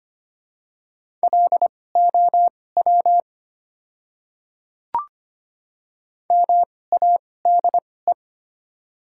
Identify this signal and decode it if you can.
Morse code, heard as a keyed tone, about 25 wpm, spaced as for 18 wpm, 700 Hz. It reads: LOW MADE
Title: 2x Character Spacing - Top 200 Words - 25wpm